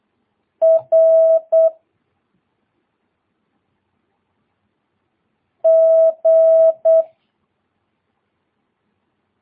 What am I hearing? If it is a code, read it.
Morse code, heard as a keyed tone, about 8 wpm, 650 Hz. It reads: R G
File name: jh8cbh / 2022cw